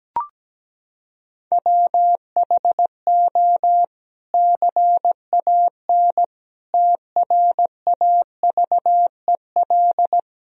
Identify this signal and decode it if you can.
Morse code, heard as a keyed tone, about 17 wpm, 700 Hz. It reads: WHO CAN TRAVEL
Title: Sentences from Top 1000 Words - Review Part 3 - 17wpm